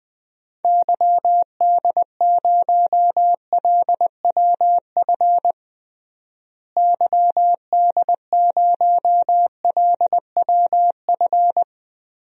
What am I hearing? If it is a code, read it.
Morse code, heard as a keyed tone, about 20 wpm, 700 Hz. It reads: YD0LWF YD0LWF
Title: DXCC-100 Call Signs - 6 Characters - Repeated 3x - 20wpm